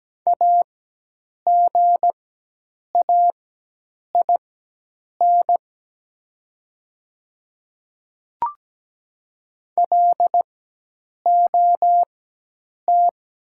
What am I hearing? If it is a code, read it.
Morse code, heard as a keyed tone, about 17 wpm, spaced as for 8 wpm, 700 Hz. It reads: AGAIN LOT
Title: 4x Character Spacing - Top 500 Words - 17wpm